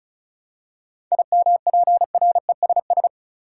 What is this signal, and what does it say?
Morse code, heard as a keyed tone, about 34 wpm, 700 Hz. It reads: IMPRESS
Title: Seven Letter Words - 35wpm